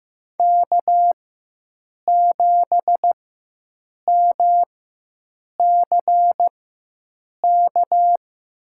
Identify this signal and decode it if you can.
Morse code, heard as a keyed tone, about 15 wpm, spaced as for 7 wpm, 700 Hz. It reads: K7MCK
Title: USA Call Signs - 5 Characters - 4x Character Spacing - 15wpm